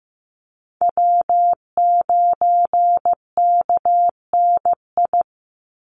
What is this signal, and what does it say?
Morse code, heard as a keyed tone, about 15 wpm, 700 Hz. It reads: W9KNI
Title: Call Signs Repeated 3x - 15wpm